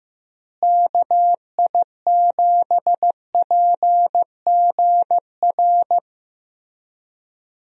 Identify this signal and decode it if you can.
Morse code, heard as a keyed tone, about 16 wpm, 700 Hz. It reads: KI7PGR
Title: USA Call Signs - 6 Characters - Repeated 3x - 15wpm